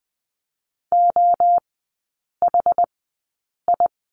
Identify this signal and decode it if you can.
Morse code, heard as a keyed tone, about 20 wpm, spaced as for 8 wpm, 700 Hz. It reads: OHI
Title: 114 Lesson 19e - Letter w - Three Characters 20wpm